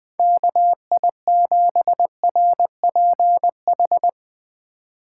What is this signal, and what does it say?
Morse code, heard as a keyed tone, about 20 wpm, 700 Hz. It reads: KI7RPH
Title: USA Call Signs - 6 Characters - 3x Character Spacing - 20wpm